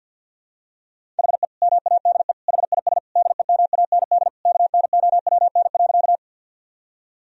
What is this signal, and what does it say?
Morse code, heard as a keyed tone, about 50 wpm, 700 Hz. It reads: HE MADE HIS DEMAND KNOWN.
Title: Sentences from Top 1000 Words - Review Part 4 - 50wpm